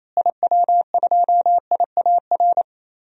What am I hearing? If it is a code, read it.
Morse code, heard as a keyed tone, about 28 wpm, 700 Hz. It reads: IW2IAR
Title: DXCC-100 Call Signs - 6 Characters - Repeated 2x - 28wpm